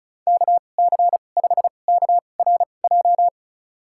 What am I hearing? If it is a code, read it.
Morse code, heard as a keyed tone, about 34 wpm, spaced as for 25 wpm, 700 Hz. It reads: KC5KRJ
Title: USA Call Signs - 6 Characters - 2x Character Spacing - 35wpm